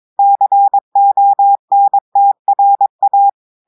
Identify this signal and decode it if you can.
Morse code, heard as a keyed tone, about 22 wpm, 800 Hz. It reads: CONTRA
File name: audios / portugais / portugais texte en clair 1 vitesse 22 mots minutes